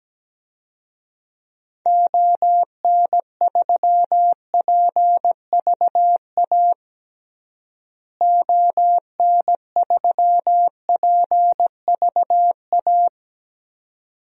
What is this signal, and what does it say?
Morse code, heard as a keyed tone, about 17 wpm, 700 Hz. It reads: ON3PVA ON3PVA
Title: DXCC-100 Call Signs - 6 Characters - Repeated 2x - 17wpm